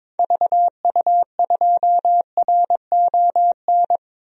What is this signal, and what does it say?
Morse code, heard as a keyed tone, about 21 wpm, 700 Hz. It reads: VU2RON
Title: DXCC-100 Call Signs - 6 Characters - 22wpm